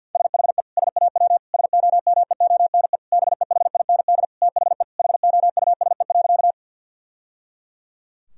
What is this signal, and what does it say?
Morse code, heard as a keyed tone, about 50 wpm, 700 Hz. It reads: SHE SAW SOMEONE BEHIND THE HOUSE.
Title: Sentences from Top 1000 Words - Review Part 4 - 50wpm